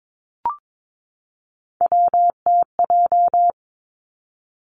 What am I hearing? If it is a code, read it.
Morse code, heard as a keyed tone, about 22 wpm, 700 Hz. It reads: WTJ
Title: Three Challenging Letter-Number Combinations 22wpm